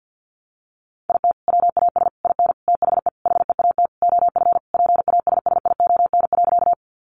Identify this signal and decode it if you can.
Morse code, heard as a keyed tone, about 50 wpm, 700 Hz. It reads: IT WAS IN THE HEAT OF PASSION.
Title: Sentences from Top 1300 Words 50wpm